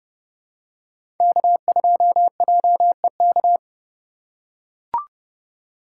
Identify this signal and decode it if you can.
Morse code, heard as a keyed tone, about 29 wpm, 700 Hz. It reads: K2JEK
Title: USA Call Signs - 5 Characters - 4x Character Spacing - 30wpm